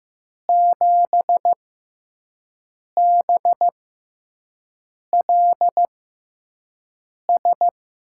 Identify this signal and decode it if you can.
Morse code, heard as a keyed tone, about 15 wpm, spaced as for 5 wpm, 700 Hz. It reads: 7BLS
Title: USA Call Signs - 5 Characters - 6x Character Spacing - 15wpm